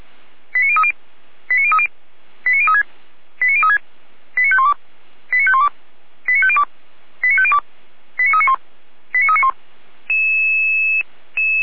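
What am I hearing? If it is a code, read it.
Five-tone selective call (ZVEI), heard as quick runs of five tones, tones of about 70 ms. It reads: 89030 89030 89036 89036 89512 89512 89502 89502 89391 89391
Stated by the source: Five-tone ZVEI selective call 89042, received on SDR.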